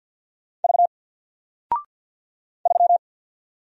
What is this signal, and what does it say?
Morse code, heard as a keyed tone, about 48 wpm, 700 Hz. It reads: V 3
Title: Single Letter-Number Alt 50wpm